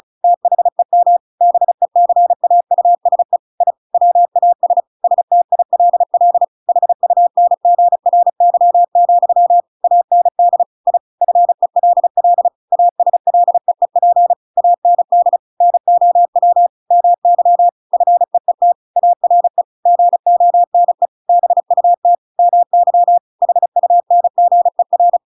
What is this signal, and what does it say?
Morse code, heard as a keyed tone, about 34 wpm, 700 Hz. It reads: THEM BECAUSE I WAS STILL HUNGRY, AND I FELL ASLEEP AND NOW MY FEET ARE GONE BUT MY HUNGER
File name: Pinocchio / Pinocchio ch7 35 35 ik1hge